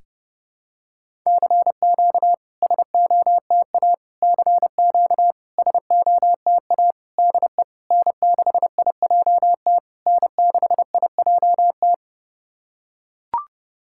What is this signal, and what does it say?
Morse code, heard as a keyed tone, about 30 wpm, 700 Hz. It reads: CQ SOTA CQ SOTA DE N6IJT N6IJT
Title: SOTA QSO Practice 30wpm